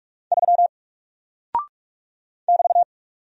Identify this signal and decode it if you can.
Morse code, heard as a keyed tone, about 44 wpm, 700 Hz. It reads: 3 <BT>
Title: Single Character (Letter:Number:Punctuation:Prosign) Alt 45wpm